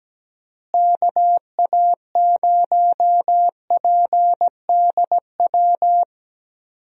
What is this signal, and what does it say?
Morse code, heard as a keyed tone, about 17 wpm, 700 Hz. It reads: KA0PDW
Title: USA Call Signs - 6 Characters - Repeated 3x - 17wpm